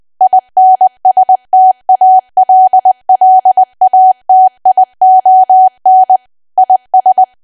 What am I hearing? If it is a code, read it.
Morse code, heard as a keyed tone, about 20 wpm, 750 Hz. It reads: INSTALLATION IS
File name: Archive / 20 WPM / 20WPM